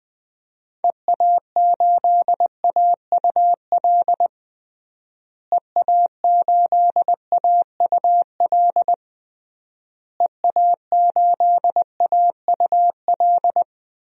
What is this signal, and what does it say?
Morse code, heard as a keyed tone, about 20 wpm, 700 Hz. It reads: EA8AUL EA8AUL EA8AUL
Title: DXCC-100 Call Signs - 6 Characters - Repeated 3x - 20wpm